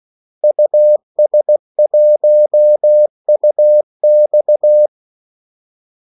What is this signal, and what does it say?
Morse code, heard as a keyed tone, about 16 wpm, 600 Hz. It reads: US1UX
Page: Kaldesignaler 16 wpm